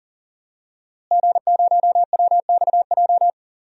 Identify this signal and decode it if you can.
Morse code, heard as a keyed tone, about 40 wpm, 700 Hz. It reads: G0WXJ
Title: DXCC-100 Call Signs - 5 Characters - Repeated 2x - 40wpm